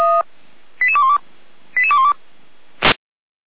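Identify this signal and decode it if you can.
Five-tone selective call (ZVEI), heard as quick runs of five tones, tones of about 70 ms. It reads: 88212 88212
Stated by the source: ZVEI five-tone selective call received via SDR.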